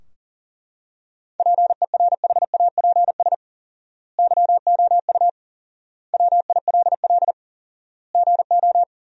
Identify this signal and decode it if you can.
Morse code, heard as a keyed tone, about 40 wpm, 700 Hz. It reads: PERHAPS YOU WILL GO
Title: 4x Word-Spacing - Sentences from Top 500 Words - 40wpm